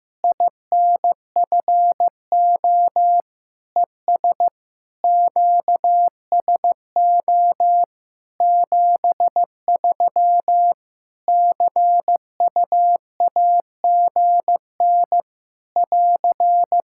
Morse code, heard as a keyed tone, about 15 wpm, 700 Hz. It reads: INFO ES QSO 73 CUAGN <AR>
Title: Practice QSO - Exchange 4 - Ending - 15wpm